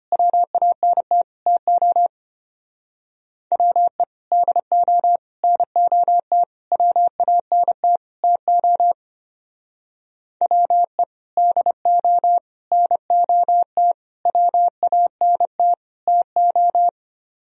Morse code, heard as a keyed tone, about 30 wpm, 700 Hz. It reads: WANT TO WE DO NOT WANT TO WE DO NOT WANT TO
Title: Speed-Racing - Sets of 5 Words 25wpm